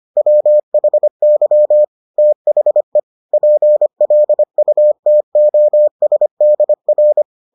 Morse code, heard as a keyed tone, about 25 wpm, 600 Hz. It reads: WHY THE PLUTOSDR